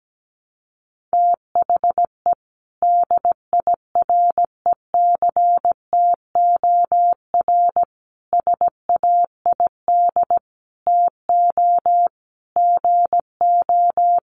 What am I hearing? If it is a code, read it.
Morse code, heard as a keyed tone, about 17 wpm, 700 Hz. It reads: THE DIRECTOR SAID TO GO
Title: Sentences from Top 1300 Words 17wpm